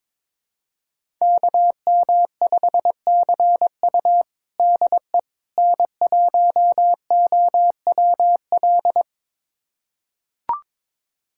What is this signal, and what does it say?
Morse code, heard as a keyed tone, about 22 wpm, 700 Hz. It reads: KM5CU DE N1OWL
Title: USA Callsign DE Callsign - 5 Characters - 22wpm